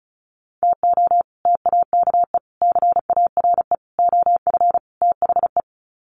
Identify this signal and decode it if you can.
Morse code, heard as a keyed tone, about 34 wpm, 700 Hz. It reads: TO TAKE CARE OF THE
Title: Sets of 5 Words Encore 35wpm